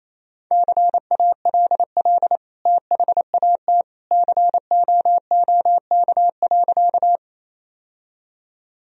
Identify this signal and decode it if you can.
Morse code, heard as a keyed tone, about 28 wpm, 700 Hz. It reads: CALL THAT COOK.
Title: Sentences from Top 1000 Words - Review Part 4 - 28wpm